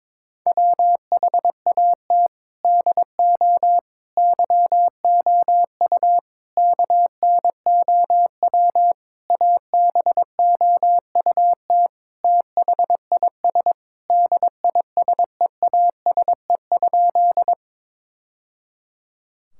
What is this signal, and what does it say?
Morse code, heard as a keyed tone, about 22 wpm, 700 Hz. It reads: WHAT DO YOU KNOW ABOUT THIS DISEASE?
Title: Sentences from Top 1000 Words - Review Part 6 - 22wpm